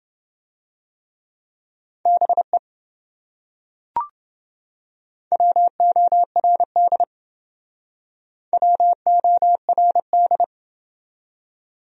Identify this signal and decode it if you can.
Morse code, heard as a keyed tone, about 28 wpm, 700 Hz. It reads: BE WORD WORD